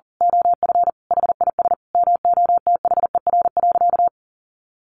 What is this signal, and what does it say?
Morse code, heard as a keyed tone, about 40 wpm, 700 Hz. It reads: OF HIS MOTHER.
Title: Sentences from Top 1300 Words 40wpm